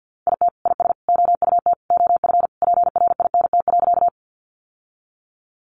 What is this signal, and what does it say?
Morse code, heard as a keyed tone, about 50 wpm, 700 Hz. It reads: IT IS OUT OF PRINT.
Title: Sentences from Top 1400 Words 50wpm